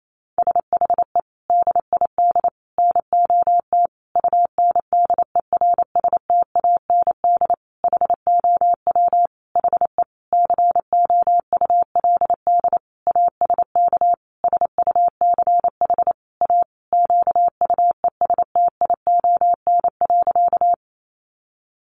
Morse code, heard as a keyed tone, about 28 wpm, 700 Hz. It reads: SHE DID NOT UNDERSTAND HOW HE COULD ASK SUCH A QUESTION.
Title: Sentences from Top 600 Words 28wpm